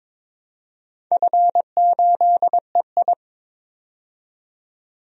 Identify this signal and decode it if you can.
Morse code, heard as a keyed tone, about 22 wpm, 700 Hz. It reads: F8EI
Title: DXCC-100 Call Signs - 4 Characters - 22wpm